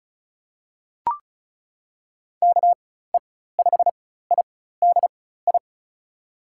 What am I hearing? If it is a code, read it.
Morse code, heard as a keyed tone, about 36 wpm, spaced as for 16 wpm, 700 Hz. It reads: KE5IDI
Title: USA Call Signs - 6 Characters - 4x Character Spacing - 35wpm